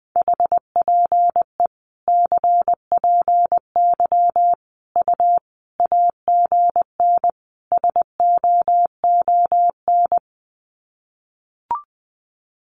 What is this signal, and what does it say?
Morse code, heard as a keyed tone, about 20 wpm, 700 Hz. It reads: HPE CPY U AGN SOON
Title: Common QSO Phrases 20wpm